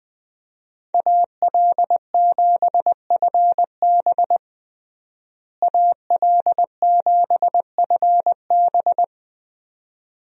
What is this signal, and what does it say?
Morse code, heard as a keyed tone, about 21 wpm, 700 Hz. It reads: AL7FB AL7FB
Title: USA Call Signs - 5 Characters - Repeated 2x - 20wpm